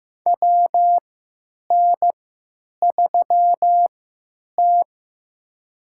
Morse code, heard as a keyed tone, about 15 wpm, spaced as for 9 wpm, 700 Hz. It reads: WN3T
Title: USA Call Signs - 3x Character Spacing - 15wpm